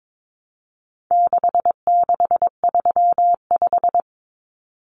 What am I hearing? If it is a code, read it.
Morse code, heard as a keyed tone, about 22 wpm, 700 Hz. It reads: 6635
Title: Numbers - 4-Digit 22wpm